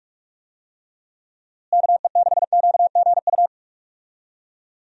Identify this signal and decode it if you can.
Morse code, heard as a keyed tone, about 42 wpm, 700 Hz. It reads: KE6QGU
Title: USA Call Signs - 6 Characters - Repeated 3x - 45wpm